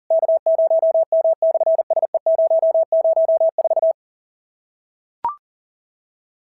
Morse code, heard as a keyed tone, about 40 wpm, 690 Hz. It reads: K0M/SE004
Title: SOTA QSO Practice 40wpm